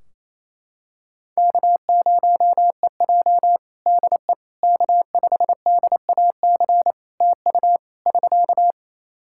Morse code, heard as a keyed tone, about 28 wpm, 700 Hz. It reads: K0EJ DE K5DAC TU <SK>
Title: Practice QSO - Exchange 4 - Ending - 28wpm